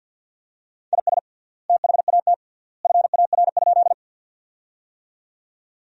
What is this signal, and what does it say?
Morse code, heard as a keyed tone, about 46 wpm, 700 Hz. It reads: IS THAT FAR?
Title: 3x Word-Spacing - Sentences from Top 300 Words - 50wpm